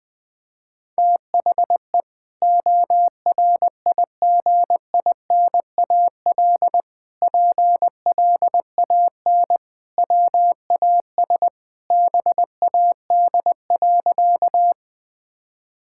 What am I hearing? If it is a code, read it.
Morse code, heard as a keyed tone, about 20 wpm, 700 Hz. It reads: THE ORIGINAL PLAN WAS BAD.
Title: Sentences from Top 1000 Words - Review Part 6 - 20wpm